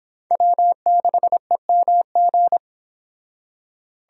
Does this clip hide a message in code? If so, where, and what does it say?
Yes: Morse code from 0.3 to 2.6 s, heard as a keyed tone, about 28 wpm, 700 Hz. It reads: W6EMG